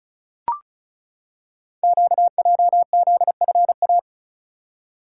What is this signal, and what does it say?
Morse code, heard as a keyed tone, about 36 wpm, 700 Hz. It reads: QJZFA
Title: Code Groups - 5 Letters - 35wpm